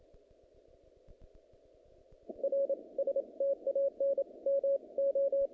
Morse code, heard as a keyed tone, about 27 wpm, 550 Hz. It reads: R STAN MO
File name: contests / naqp-cw-k5tr-op-k5pi- -2200